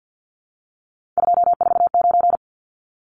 Morse code, heard as a keyed tone, about 50 wpm, 700 Hz. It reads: 249
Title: Numbers - 3-Digit 50wpm